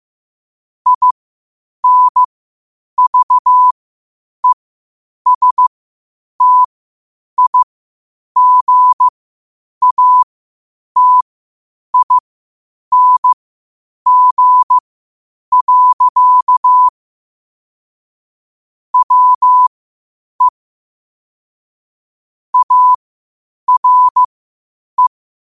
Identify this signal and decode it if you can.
Morse code, heard as a keyed tone, about 15 wpm, spaced as for 8 wpm, 1000 Hz. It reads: INVESTIGATING. WE ARE
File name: qotdfiles / 8wpm